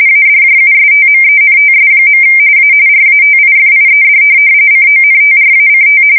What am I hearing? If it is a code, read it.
RTTY, about 45.45 baud. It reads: ADJUSTMENTS. A RATIFIED CONTRACT
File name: fsk170 / teletype-corp